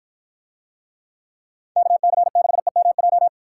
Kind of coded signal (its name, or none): Morse code